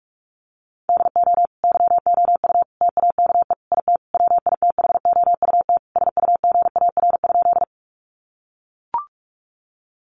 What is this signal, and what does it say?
Morse code, heard as a keyed tone, about 44 wpm, 700 Hz. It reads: DO YOU TAKE IT WITHOUT SUGAR?
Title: Sentences from Top 1200 Words 45wpm